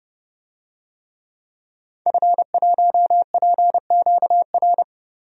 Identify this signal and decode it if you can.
Morse code, heard as a keyed tone, about 30 wpm, 700 Hz. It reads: F1PQR
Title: DXCC-100 Call Signs - 5 Characters - 30wpm